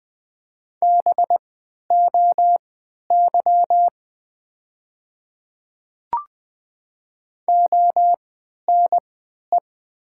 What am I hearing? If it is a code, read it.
Morse code, heard as a keyed tone, about 20 wpm, spaced as for 11 wpm, 700 Hz. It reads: BOY ONE
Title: 3x Character Spacing - Top 200 Words - 20wpm